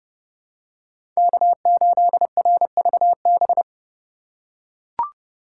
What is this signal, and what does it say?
Morse code, heard as a keyed tone, about 29 wpm, 700 Hz. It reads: K8RVB